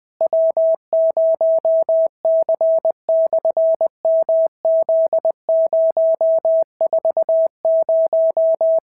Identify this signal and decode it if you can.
Morse code, heard as a keyed tone, about 20 wpm, 650 Hz. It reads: W0C/MZ040
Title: SOTA QSO Practice 20wpm